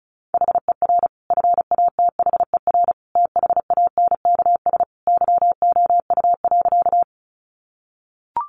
Morse code, heard as a keyed tone, about 34 wpm, 700 Hz. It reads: HER FATHER THANKS YOU.
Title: Sentences from Top 600 Words 35wpm